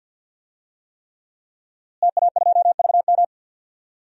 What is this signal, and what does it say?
Morse code, heard as a keyed tone, about 51 wpm, 700 Hz. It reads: TA2VM
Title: DXCC-100 Call Signs - 5 Characters - Repeated 3x - 50wpm